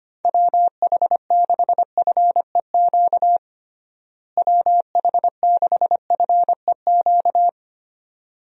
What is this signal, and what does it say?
Morse code, heard as a keyed tone, about 25 wpm, 700 Hz. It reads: WH6FEQ WH6FEQ
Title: USA Call Signs - 6 Characters - Repeated 2x - 25wpm